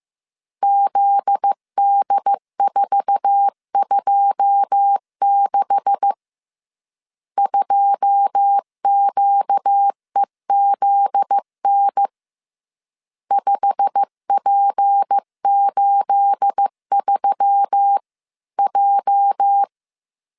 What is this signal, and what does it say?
Morse code, heard as a keyed tone, about 15 wpm, 800 Hz. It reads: ZD426 2QEZN 5P83J
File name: CW / DET / 1 9 1500C S A 14wpm 3222